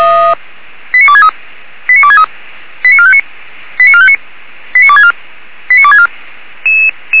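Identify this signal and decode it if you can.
Five-tone selective call (ZVEI), heard as quick runs of five tones, tones of about 70 ms. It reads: 89263 89263 89459 89459 89264 89264
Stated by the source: ZVEI five-tone call received on SDR.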